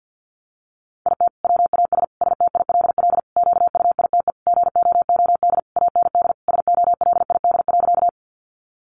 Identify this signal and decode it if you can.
Morse code, heard as a keyed tone, about 50 wpm, 700 Hz. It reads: IT WAS STILL QUITE GOOD AND SOLID.
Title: Sentences from Top 1500 Words 50wpm